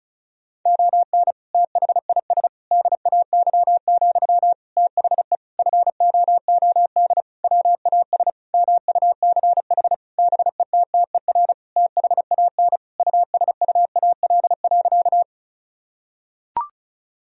Morse code, heard as a keyed tone, about 34 wpm, 700 Hz. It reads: ON THIS DAY, THE FOOD WAS MUCH BETTER THAN USUAL.
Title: Sentences from Top 1000 Words - Review Part 2 - 35wpm